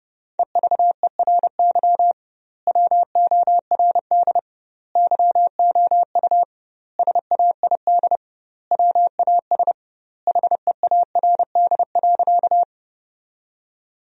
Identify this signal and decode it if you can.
Morse code, heard as a keyed tone, about 30 wpm, 700 Hz. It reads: EVERY WORD YOU SAID WAS HEARD.
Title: 2x Word-Spacing - Sentences from Top 400 Words - 30wpm